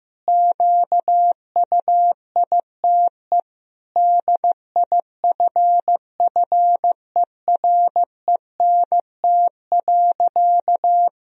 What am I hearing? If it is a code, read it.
Morse code, heard as a keyed tone, about 15 wpm, 700 Hz. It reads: QUITE DIFFERENT.
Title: Sentences from Top 1000 Words - Review Part 3 - 15wpm